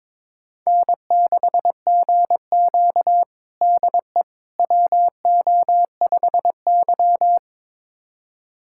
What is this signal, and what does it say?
Morse code, heard as a keyed tone, about 22 wpm, 700 Hz. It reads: N6GQ DE WO5Y
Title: USA Callsign DE Callsign - 4 Characters - 22wpm